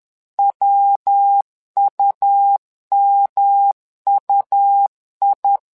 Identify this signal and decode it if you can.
Morse code, heard as a keyed tone, about 10 wpm, 800 Hz. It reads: WUMUI
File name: LEZIONI / 3 MWRO / 3 MWRO 10wpm 2148